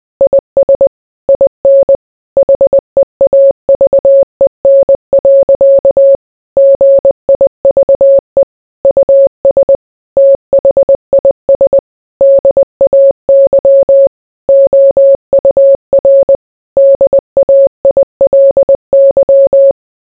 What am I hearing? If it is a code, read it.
Morse code, heard as a keyed tone, about 20 wpm, 550 Hz. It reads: IS IN HEAVEN. GIVE US THIS DAY OUR DAILY